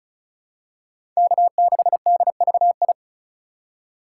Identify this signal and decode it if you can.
Morse code, heard as a keyed tone, about 36 wpm, 700 Hz. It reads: K6DVI
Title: USA Call Signs - 5 Characters - 3x Character Spacing - 35wpm